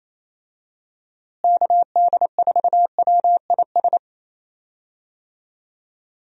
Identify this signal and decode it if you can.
Morse code, heard as a keyed tone, about 28 wpm, 700 Hz. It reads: KD4WIS
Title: USA Call Signs - 6 Characters - Repeated 2x - 28wpm